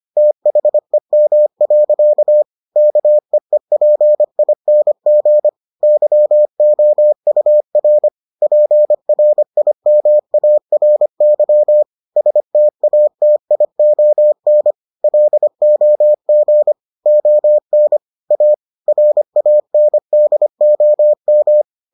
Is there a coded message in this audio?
Yes: Morse code from 0.2 to 21.6 s, heard as a keyed tone, about 25 wpm, 600 Hz. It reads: THEM. KEEPING YOUR PRIMARY STATION LOG ON A RANDOM